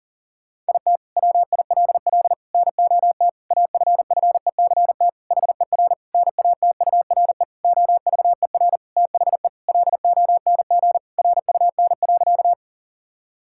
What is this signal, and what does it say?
Morse code, heard as a keyed tone, about 37 wpm, 700 Hz. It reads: IT WILL NOT AFFECT HER NATURE OVER THE LONG RUN.
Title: Sentences from Top 1000 Words - Review Part 6 - 40wpm